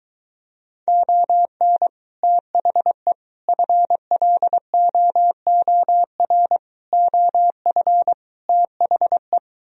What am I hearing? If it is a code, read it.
Morse code, heard as a keyed tone, about 23 wpm, 700 Hz. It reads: ON THE FLOOR OF THE